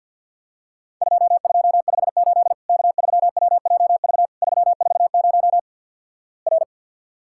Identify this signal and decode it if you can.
Morse code, heard as a keyed tone, about 50 wpm, 700 Hz. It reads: 2258 K3WJV 340 R
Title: Worked All Europe DX Contest - QTC Exchange - 50wpm